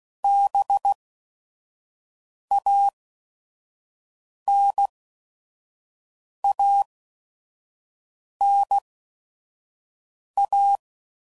Morse code, heard as a keyed tone, about 16 wpm, spaced as for 5 wpm, 800 Hz. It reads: BANANA